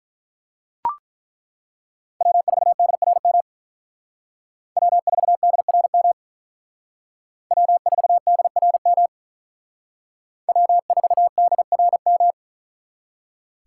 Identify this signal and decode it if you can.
Morse code, heard as a keyed tone, about 42 wpm, 700 Hz. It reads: W4DRM W4DRM W4DRM W4DRM
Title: Speed Racing - USA 1x3 Call Signs 35wpm